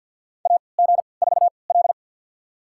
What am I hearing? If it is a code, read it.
Morse code, heard as a keyed tone, about 47 wpm, spaced as for 27 wpm, 700 Hz. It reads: AG4L
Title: USA Call Signs - 3x Character Spacing - 50wpm